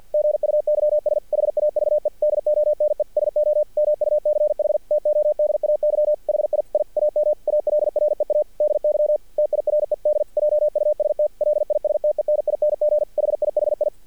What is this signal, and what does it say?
Morse code, heard as a keyed tone, about 50 wpm, 600 Hz. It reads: GUYS HAVE DONE SO MUCH TODAY HI I AM ALREADY TIRED JUST LISTENING HIHI
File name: archived / CRSnet